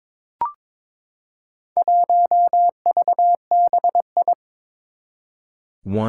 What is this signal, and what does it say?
Morse code, heard as a keyed tone, about 22 wpm, 700 Hz. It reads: 1VBI
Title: Four Challenging Letter-Number Combinations 22wpm